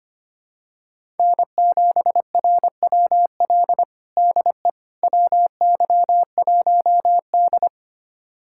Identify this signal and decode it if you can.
Morse code, heard as a keyed tone, about 25 wpm, 700 Hz. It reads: N7RWL DE WY1D
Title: USA Callsign DE Callsign 25wpm